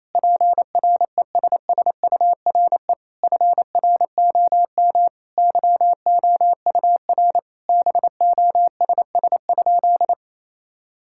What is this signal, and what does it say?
Morse code, heard as a keyed tone, about 28 wpm, 700 Hz. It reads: PRESSURE FROM YOUR BOSS?
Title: Sentences from Top 1000 Words - Review Part 6 - 28wpm